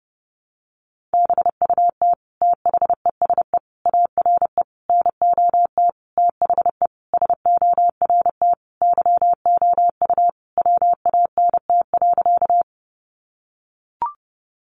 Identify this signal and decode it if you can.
Morse code, heard as a keyed tone, about 30 wpm, 700 Hz. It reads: BUT THESE ARE NOT THE SORT YOU WANT.
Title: Sentences from Top 1300 Words 30wpm